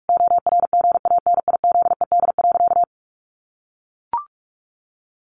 Morse code, heard as a keyed tone, about 44 wpm, 700 Hz. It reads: ORGANIZED.
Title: Sentences from Top 1300 Words 45wpm